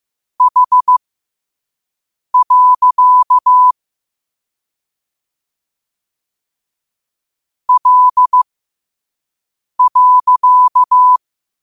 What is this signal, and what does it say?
Morse code, heard as a keyed tone, about 15 wpm, spaced as for 5 wpm, 1000 Hz. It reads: H. L.
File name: 2026-02-15-5wpm